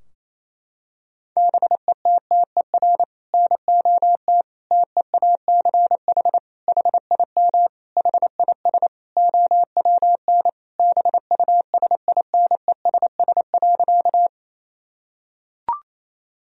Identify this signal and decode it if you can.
Morse code, heard as a keyed tone, about 27 wpm, 700 Hz. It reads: BETTER NOT TEACH HIM HIS OWN BUSINESS.
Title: Sentences from Top 1000 Words - Review Part 2 - 28wpm